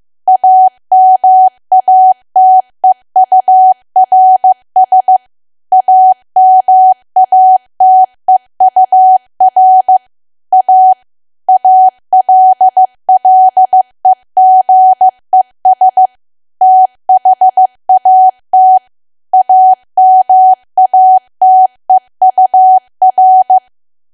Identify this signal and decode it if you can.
Morse code, heard as a keyed tone, about 15 wpm, 750 Hz. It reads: AMATEURS AMATEUR A ALLEGES THAT AMATEUR
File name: cwa / W1AW-Level2 / 15WPM